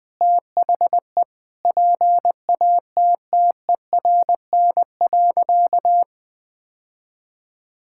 Morse code, heard as a keyed tone, about 20 wpm, 700 Hz. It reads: THE PATTERN.
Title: Sentences with 5-Word Phrases - Spoken Beginning - 20wpm